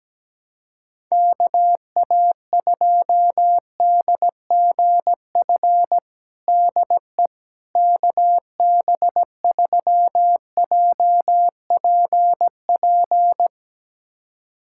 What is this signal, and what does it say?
Morse code, heard as a keyed tone, about 17 wpm, 700 Hz. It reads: KA2DGF DE KB3JPP
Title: USA Callsign DE Callsign - 6 Characters - 17wpm